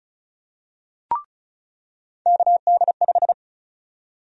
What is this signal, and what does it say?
Morse code, heard as a keyed tone, about 36 wpm, 700 Hz. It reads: KD5
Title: USA Call Sign Prefixes - 35wpm